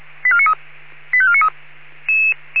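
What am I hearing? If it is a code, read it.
Five-tone selective call (ZVEI), heard as quick runs of five tones, tones of about 70 ms. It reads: 85493
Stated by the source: ZVEI five-tone selective call received on SDR.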